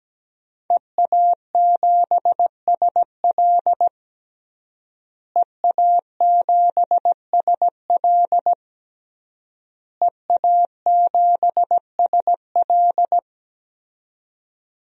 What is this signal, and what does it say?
Morse code, heard as a keyed tone, about 17 wpm, 700 Hz. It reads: EA7SL EA7SL EA7SL
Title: DXCC-100 Call Signs - 5 Characters - Repeated 3x - 17wpm